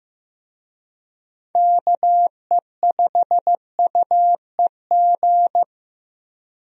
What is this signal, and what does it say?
Morse code, heard as a keyed tone, about 15 wpm, 700 Hz. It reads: KE5UEG
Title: USA Call Signs - 6 Characters - Repeated 2x - 15wpm